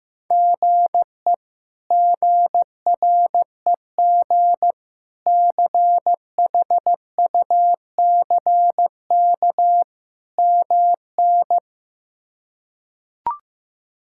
Morse code, heard as a keyed tone, about 15 wpm, 700 Hz. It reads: GE GREG CHUCK MN
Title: K1USN SST QSO Practice - 15wpm